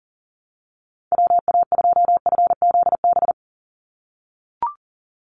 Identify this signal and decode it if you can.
Morse code, heard as a keyed tone, about 40 wpm, 700 Hz. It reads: WA2FZB
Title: Call Signs - 40wpm